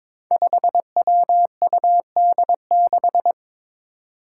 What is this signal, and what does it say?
Morse code, heard as a keyed tone, about 22 wpm, 700 Hz. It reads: 5WUD6
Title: Five Challenging Letter-Number Combinations 22wpm